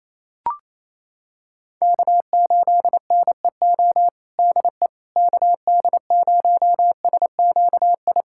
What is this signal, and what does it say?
Morse code, heard as a keyed tone, about 28 wpm, 700 Hz. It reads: K8NEO DE KD0SQI
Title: USA Callsign DE Callsign 28wpm